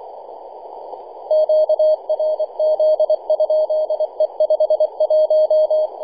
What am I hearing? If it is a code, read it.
Morse code, heard as a keyed tone, about 25 wpm, 620 Hz. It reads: QRZ? E51
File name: e51dwc-21-010mhz-cw